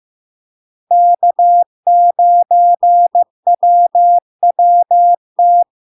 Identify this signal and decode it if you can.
Morse code, heard as a keyed tone, about 15 wpm, 700 Hz. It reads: K9WWT
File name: LARK-FD-2022-Log-in-CW-15wpm-700-Hz